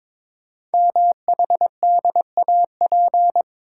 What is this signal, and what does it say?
Morse code, heard as a keyed tone, about 22 wpm, 700 Hz. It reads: MHDAP
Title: Code Groups - 5 Letters - 22wpm